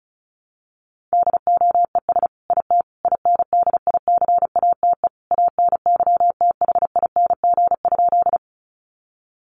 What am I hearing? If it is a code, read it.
Morse code, heard as a keyed tone, about 34 wpm, 700 Hz. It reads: DOES IT INDICATE ANYTHING?
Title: Sentences from Top 1300 Words 35wpm